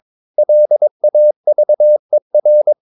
Morse code, heard as a keyed tone, about 22 wpm, 600 Hz. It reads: LAVER
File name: Ferie QRS 22wpm